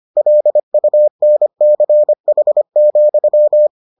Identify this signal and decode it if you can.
Morse code, heard as a keyed tone, about 25 wpm, 600 Hz. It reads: LUNCH,